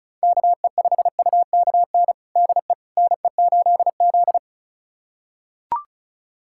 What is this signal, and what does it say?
Morse code, heard as a keyed tone, about 36 wpm, 700 Hz. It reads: KE5UKN DE NE8Z
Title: USA Callsign DE Callsign 35wpm